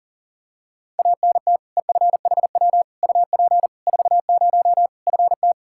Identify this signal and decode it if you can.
Morse code, heard as a keyed tone, about 40 wpm, 700 Hz. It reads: ANT EFHW UP 40 FT
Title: K1USN SST QSO Practice 40wpm